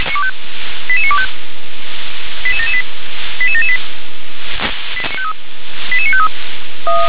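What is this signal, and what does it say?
Five-tone selective call (ZVEI), heard as quick runs of five tones, tones of about 70 ms. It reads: 88925 88778 88778 88953
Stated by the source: ZVEI five-tone selective call received on an SDR.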